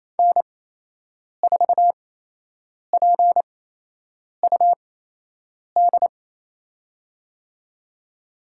Morse code, heard as a keyed tone, about 28 wpm, spaced as for 8 wpm, 700 Hz. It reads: N4PUD
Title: USA Call Signs - 8x Character Spacing - 28wpm